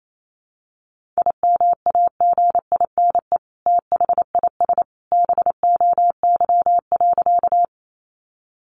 Morse code, heard as a keyed tone, about 28 wpm, 700 Hz. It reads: IMAGINE THIS BOY.
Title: Sentences from Top 1400 Words 28wpm